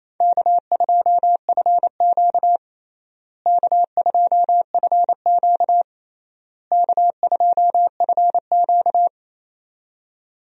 Morse code, heard as a keyed tone, about 28 wpm, 700 Hz. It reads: K2FQ K2FQ K2FQ
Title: USA Call Signs - 4 Characters - Repeated 3x - 28wpm